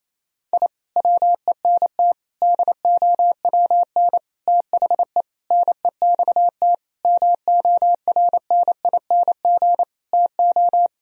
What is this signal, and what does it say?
Morse code, heard as a keyed tone, about 28 wpm, 700 Hz. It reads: I WENT DOWN THE NEXT MORNING TO
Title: Sentences from Top 1000 Words - Review Part 1 - 28wpm